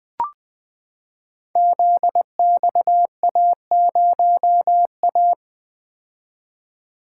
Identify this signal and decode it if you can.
Morse code, heard as a keyed tone, about 21 wpm, 700 Hz. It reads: ZXA0A
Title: Code Groups - 5 Letter:Numbers - 20wpm